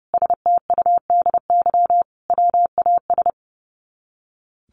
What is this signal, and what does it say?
Morse code, heard as a keyed tone, about 30 wpm, 700 Hz. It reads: STUDY WAS
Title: Sets of 5 Words 30wpm